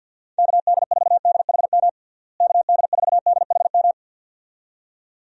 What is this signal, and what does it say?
Morse code, heard as a keyed tone, about 51 wpm, 700 Hz. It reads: KD4DHM KD4DHM
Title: USA Call Signs - 6 Characters - Repeated 2x - 50wpm